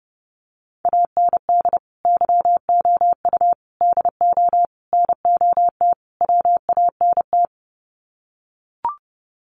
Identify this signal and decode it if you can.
Morse code, heard as a keyed tone, about 30 wpm, 700 Hz. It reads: AND YOU DO NOT WANT
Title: Sets of 5 Words Encore2 30wpm